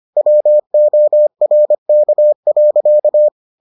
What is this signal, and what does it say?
Morse code, heard as a keyed tone, about 25 wpm, 600 Hz. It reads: WORK.